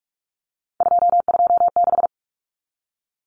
Morse code, heard as a keyed tone, about 44 wpm, 700 Hz. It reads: 226
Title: Common Cut Numbers - 3-Digit 45wpm